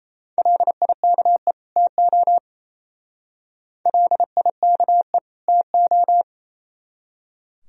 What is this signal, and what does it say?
Morse code, heard as a keyed tone, about 35 wpm, 700 Hz. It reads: LIKE TO LIKE TO